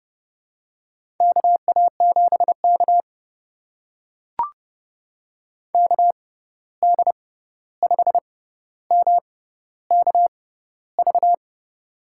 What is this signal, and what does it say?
Morse code, heard as a keyed tone, about 30 wpm, spaced as for 14 wpm, 700 Hz. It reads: KA7K KD5MKV
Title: USA Call Signs - 6x Character Spacing - 30wpm